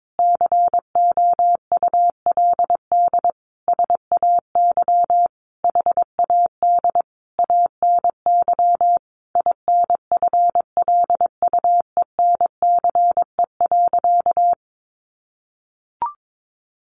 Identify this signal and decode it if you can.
Morse code, heard as a keyed tone, about 22 wpm, 700 Hz. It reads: COULD SAY HAD ANY INFLUENCE.
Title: Sentences from Top 1100 Words 22wpm